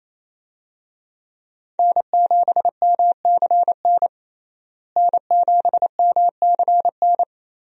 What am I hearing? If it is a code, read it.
Morse code, heard as a keyed tone, about 28 wpm, 700 Hz. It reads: N7MCN N7MCN
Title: USA Call Signs - 5 Characters - Repeated 2x - 28wpm